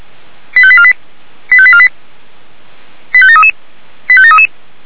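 Five-tone selective call (ZVEI), heard as quick runs of five tones, tones of about 70 ms. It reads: 85848 85848 85733 85733
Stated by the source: ZVEI five-tone call received by SDR.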